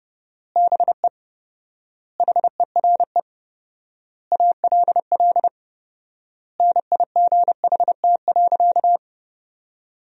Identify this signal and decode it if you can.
Morse code, heard as a keyed tone, about 30 wpm, 700 Hz. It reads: BE HERE ALL NIGHT.
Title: 4x Word-Spacing - Sentences from Top 300 Words - 30wpm